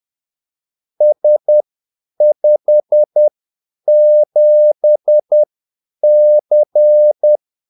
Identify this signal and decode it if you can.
Morse code, heard as a keyed tone, about 10 wpm, spaced as for 8 wpm, 600 Hz. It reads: S57C
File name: Callsigns-10-8wpm